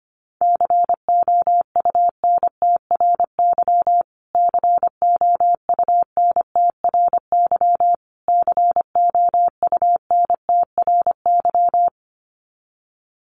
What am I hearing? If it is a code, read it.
Morse code, heard as a keyed tone, about 25 wpm, 700 Hz. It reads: COUNTRY COUNTRY COUNTRY
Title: Top 500 Words Repeated 25wpm